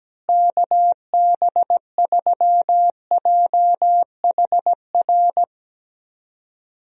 Morse code, heard as a keyed tone, about 17 wpm, 700 Hz. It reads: KB3JHR
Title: USA Call Signs - 6 Characters - Repeated 2x - 17wpm